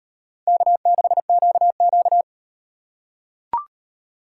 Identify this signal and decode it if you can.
Morse code, heard as a keyed tone, about 35 wpm, 700 Hz. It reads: K6QQ